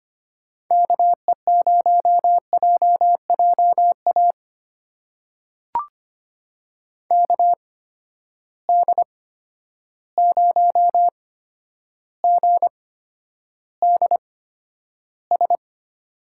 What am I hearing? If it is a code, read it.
Morse code, heard as a keyed tone, about 25 wpm, spaced as for 11 wpm, 700 Hz. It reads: KE0JJA KD0GDS
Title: USA Call Signs - 6 Characters - 8x Character Spacing - 25wpm